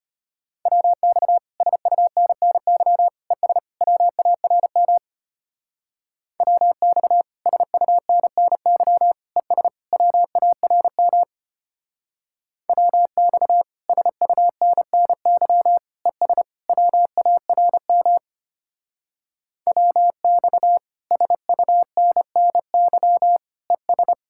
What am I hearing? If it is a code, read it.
Morse code, heard as a keyed tone, about 30 wpm, 700 Hz. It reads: WX SUNNY ES WARM WX SUNNY ES WARM WX SUNNY ES WARM WX SUNNY ES
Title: Speed-Racing Common QSO Phrases Encore - 25wpm